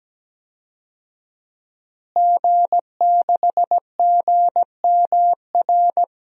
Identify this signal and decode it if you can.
Morse code, heard as a keyed tone, about 17 wpm, 700 Hz. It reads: G6GMR
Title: DXCC-100 Call Signs - 5 Characters - Repeated 2x - 17wpm